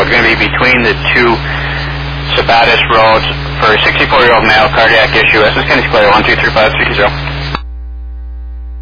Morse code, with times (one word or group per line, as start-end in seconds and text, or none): none